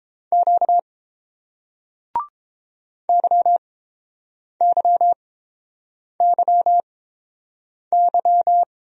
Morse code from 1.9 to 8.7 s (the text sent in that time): YYYY